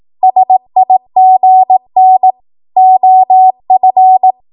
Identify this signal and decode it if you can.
Morse code, heard as a keyed tone, about 18 wpm, 750 Hz. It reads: SIGN OF
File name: Morse / 18WPM